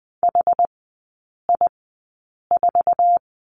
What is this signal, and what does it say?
Morse code, heard as a keyed tone, about 20 wpm, spaced as for 8 wpm, 700 Hz. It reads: HI4
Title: 060 Lesson 11e - Letter h - Three Characters 20wpm